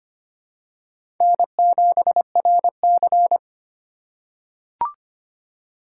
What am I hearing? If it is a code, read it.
Morse code, heard as a keyed tone, about 25 wpm, 700 Hz. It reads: N7RC
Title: USA Call Signs - 3x Character Spacing - 25wpm